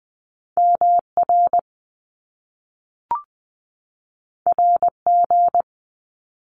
Morse code, heard as a keyed tone, about 20 wpm, 700 Hz. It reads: MR RG
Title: Top Two Letter Combinations in Words 20wpm